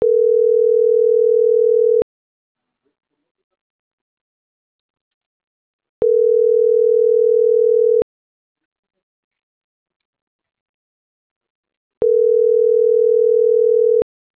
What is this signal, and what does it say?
Ringback tone (North America), 440 + 480 Hz.